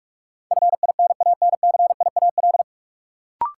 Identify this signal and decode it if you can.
Morse code, heard as a keyed tone, about 43 wpm, 700 Hz. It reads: FINANCIAL
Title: Sentences from Top 1000 Words - Review Part 5 - New Word Drill - 45wpm